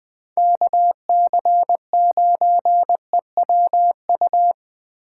Morse code, heard as a keyed tone, about 20 wpm, 700 Hz. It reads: KC9EWU